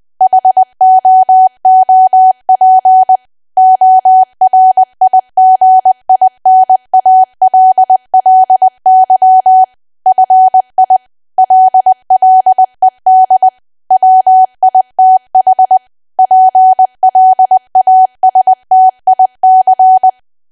Morse code, heard as a keyed tone, about 20 wpm, 750 Hz. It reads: HOOP ORIGINALLY FILLED WITH PLASTIC
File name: cwa / W1AW-Level3 / 20WPM